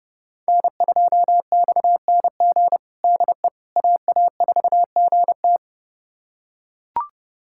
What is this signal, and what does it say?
Morse code, heard as a keyed tone, about 30 wpm, 700 Hz. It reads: N2XNG DE AA4GT
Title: USA Callsign DE Callsign - 5 Characters - 30wpm